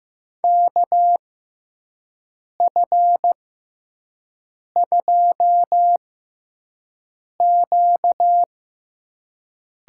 Morse code, heard as a keyed tone, about 15 wpm, spaced as for 5 wpm, 700 Hz. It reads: KF2Q
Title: USA Call Signs - 5 Characters - 6x Character Spacing - 15wpm